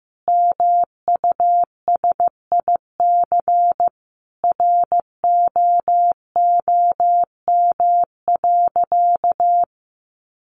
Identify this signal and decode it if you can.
Morse code, heard as a keyed tone, about 15 wpm, 700 Hz. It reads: MUSIC ROOM.
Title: Sentences from Top 700 Words 15wpm